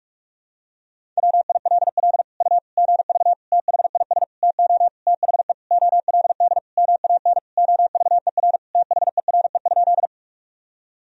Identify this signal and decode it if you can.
Morse code, heard as a keyed tone, about 41 wpm, 700 Hz. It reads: WILL U GV THIS TO THE OLD MAN OVER THERE?
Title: Sentences from Top 200 Words and Abbreviations 45wpm